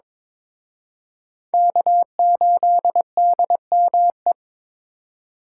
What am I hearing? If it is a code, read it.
Morse code, heard as a keyed tone, about 26 wpm, 700 Hz. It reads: K8DME
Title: Speed Racing - USA Call Signs 22wpm